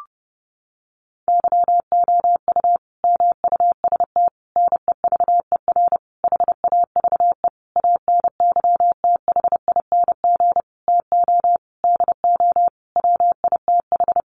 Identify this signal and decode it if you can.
Morse code, heard as a keyed tone, about 30 wpm, 700 Hz. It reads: YOU MUST NEVER HAVE ANYTHING TO DO WITH
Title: Sentences from Top 1000 Words 30wpm